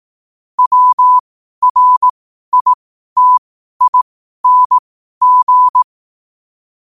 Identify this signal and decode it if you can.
Morse code, heard as a keyed tone, about 18 wpm, spaced as for 12 wpm, 1000 Hz. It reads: WRITING
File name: qotdfiles / 12wpm